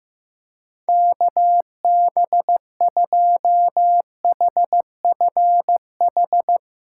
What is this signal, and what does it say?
Morse code, heard as a keyed tone, about 15 wpm, 700 Hz. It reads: KB2HFH
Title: USA Callsign DE Callsign - 6 Characters - 15wpm